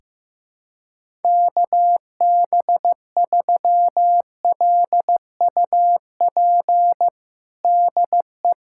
Morse code, heard as a keyed tone, about 15 wpm, 700 Hz. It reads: KB3LUP DE
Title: USA Callsign DE Callsign 15wpm